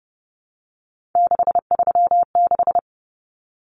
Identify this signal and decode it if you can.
Morse code, heard as a keyed tone, about 30 wpm, 700 Hz. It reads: 636
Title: Numbers - 3-Digit 30wpm